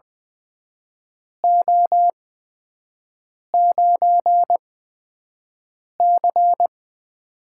Morse code, heard as a keyed tone, about 20 wpm, spaced as for 5 wpm, 700 Hz. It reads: O9C
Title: USA Call Signs - 8x Character Spacing - 20wpm